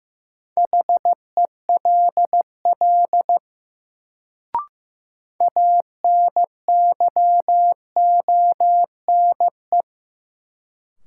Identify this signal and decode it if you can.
Morse code, heard as a keyed tone, about 15 wpm, 700 Hz. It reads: HELL ANYONE
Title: Sets of 5 Words Encore - New Word Drill - 15wpm